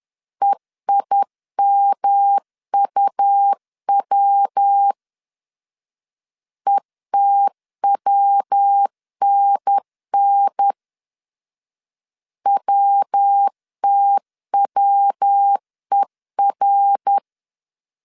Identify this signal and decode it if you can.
Morse code, heard as a keyed tone, about 10 wpm, 800 Hz. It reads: EIMUW ETWNN WTWER
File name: CW / DET / 1 3 1500C S B 10wpm 3335